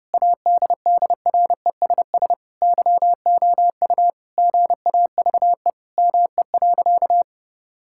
Morse code, heard as a keyed tone, about 32 wpm, 700 Hz. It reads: ADDRESS YOU GAVE ME.
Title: Sentences from Top 1000 Words - Review Part 6 - 30wpm